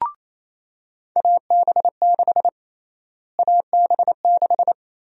Morse code, heard as a keyed tone, about 28 wpm, 700 Hz. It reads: AB6 AB6
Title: USA Call Sign Prefixes - Repeated 2x - 28wpm